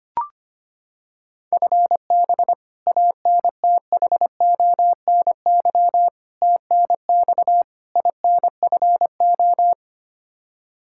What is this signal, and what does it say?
Morse code, heard as a keyed tone, about 25 wpm, 700 Hz. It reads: FB ANTHONY TNX INFO
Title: Practice QSO - Exchange 3 - More Info - 25wpm